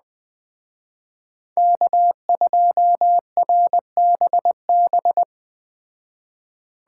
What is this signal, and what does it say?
Morse code, heard as a keyed tone, about 20 wpm, 700 Hz. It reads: K2RBB
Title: Speed Racing - USA 1x3 Call Signs 17wpm